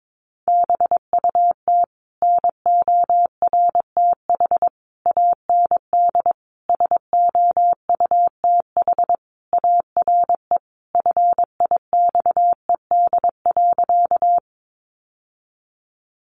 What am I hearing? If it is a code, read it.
Morse code, heard as a keyed tone, about 22 wpm, 700 Hz. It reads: BUT NORTH AND SOUTH ARE FIXED.
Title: Sentences from Top 1100 Words 22wpm